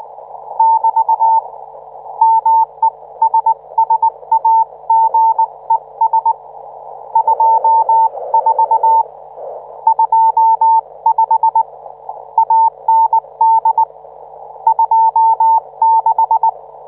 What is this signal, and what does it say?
Morse code, heard as a keyed tone, about 19 wpm, 900 Hz. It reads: <BT> MESSAGES 24 25 AND 26